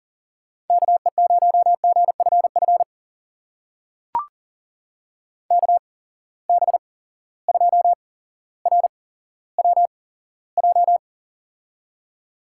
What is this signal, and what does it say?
Morse code, heard as a keyed tone, about 40 wpm, spaced as for 17 wpm, 700 Hz. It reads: KE0GFF KB2RWJ
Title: USA Call Signs - 6 Characters - 8x Character Spacing - 40wpm